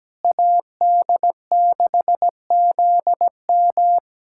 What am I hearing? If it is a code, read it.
Morse code, heard as a keyed tone, about 17 wpm, 700 Hz. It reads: AD6ZM
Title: USA Callsign DE Callsign - 5 Characters - 17wpm